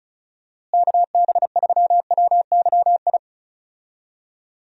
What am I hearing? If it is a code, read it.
Morse code, heard as a keyed tone, about 34 wpm, 700 Hz. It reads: KB3WYI
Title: USA Call Signs - 6 Characters - Repeated 3x - 35wpm